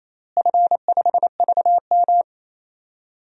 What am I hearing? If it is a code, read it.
Morse code, heard as a keyed tone, about 27 wpm, 700 Hz. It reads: F5VM
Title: DXCC-100 Call Signs - 4 Characters - Repeated 3x - 28wpm